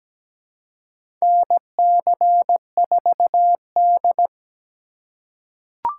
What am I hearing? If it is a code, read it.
Morse code, heard as a keyed tone, about 17 wpm, 700 Hz. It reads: NC4D
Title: USA Call Signs - 4 Characters - 2x Character Spacing - 17wpm